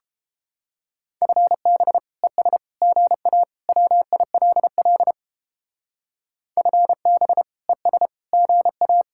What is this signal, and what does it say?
Morse code, heard as a keyed tone, about 31 wpm, 700 Hz. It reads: FB ES GA WILL FB ES GA